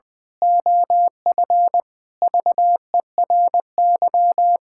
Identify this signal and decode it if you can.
Morse code, heard as a keyed tone, about 20 wpm, 700 Hz. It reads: OF VERY
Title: Pangram Challenge v2 - 20wpm